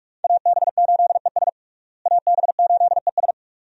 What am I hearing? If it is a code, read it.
Morse code, heard as a keyed tone, about 43 wpm, 700 Hz. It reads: AB8ES AB8ES
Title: USA Call Signs - 5 Characters - Repeated 2x - 45wpm